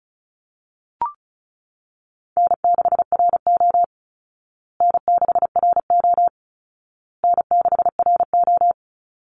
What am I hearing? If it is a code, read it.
Morse code, heard as a keyed tone, about 34 wpm, 700 Hz. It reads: N6RO N6RO N6RO
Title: Easy Call Signs Repeated 3x - 35wpm